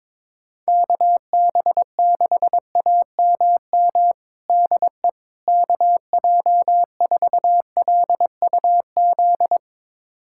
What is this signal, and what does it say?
Morse code, heard as a keyed tone, about 22 wpm, 700 Hz. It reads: KB6AMM DE KJ4LUZ
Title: USA Callsign DE Callsign - 6 Characters - 22wpm